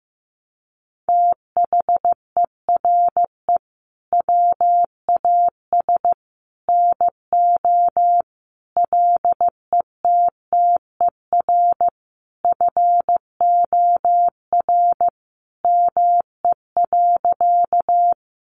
Morse code, heard as a keyed tone, about 15 wpm, 700 Hz. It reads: THERE WAS NO LETTER FOR ME.
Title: Sentences from Top 300 Words 15wpm